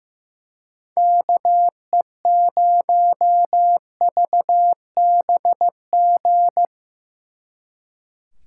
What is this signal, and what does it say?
Morse code, heard as a keyed tone, about 16 wpm, 700 Hz. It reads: KE0VBG
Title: Speed Racing - USA Call Signs 15wpm